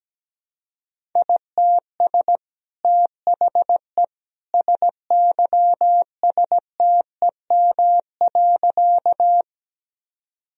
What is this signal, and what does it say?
Morse code, heard as a keyed tone, about 17 wpm, 700 Hz. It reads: ITS THE SYSTEM.
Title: Sentences from Top 1000 Words - Review Part 3 - 17wpm